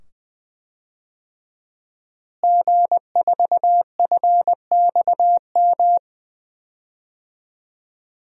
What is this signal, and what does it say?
Morse code, heard as a keyed tone, about 20 wpm, 700 Hz. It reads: G4FXM
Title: DXCC-100 Call Signs - 5 Characters - Repeated 2x - 20wpm